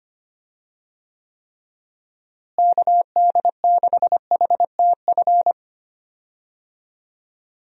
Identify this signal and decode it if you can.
Morse code, heard as a keyed tone, about 25 wpm, 700 Hz. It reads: KD6HTF
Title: USA Call Signs - 6 Characters - Repeated 2x - 25wpm